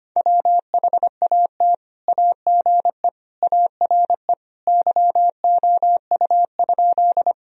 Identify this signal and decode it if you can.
Morse code, heard as a keyed tone, about 25 wpm, 700 Hz. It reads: WHAT AGE ARE YOU?
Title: Sentences from Top 1000 Words - Review Part 3 - 25wpm